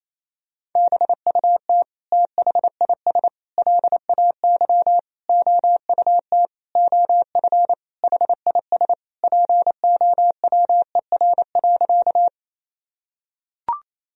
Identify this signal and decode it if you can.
Morse code, heard as a keyed tone, about 28 wpm, 700 Hz. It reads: BUT THIS LAY OUT OF HIS POWER.
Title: Sentences from Top 1000 Words - Review Part 2 - 28wpm